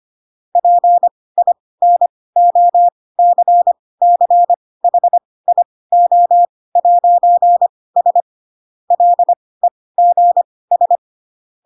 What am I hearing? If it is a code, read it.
Morse code, heard as a keyed tone, about 25 wpm, spaced as for 18 wpm, 700 Hz. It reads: PINOCCHIO'S LEGS
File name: Pinocchio ch3 12 25 ik1hge